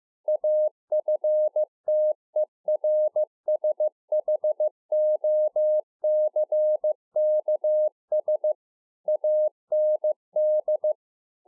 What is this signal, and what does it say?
Morse code, heard as a keyed tone, about 15 wpm, 610 Hz. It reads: AFTERSHOCKS AND